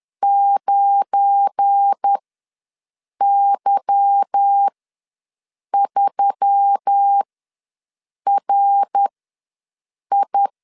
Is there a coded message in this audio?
Yes: Morse code from 0.2 to 10.5 s, heard as a keyed tone, about 10 wpm, spaced as for 6 wpm, 800 Hz. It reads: 9Y3RI